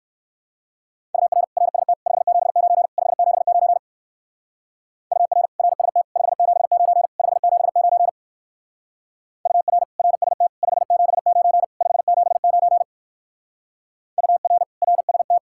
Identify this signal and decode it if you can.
Morse code, heard as a keyed tone, about 54 wpm, 700 Hz. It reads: UR RST 579 579 UR RST 579 579 UR RST 579 579 UR RST
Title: Speed-Racing Common QSO Phrases - 45wpm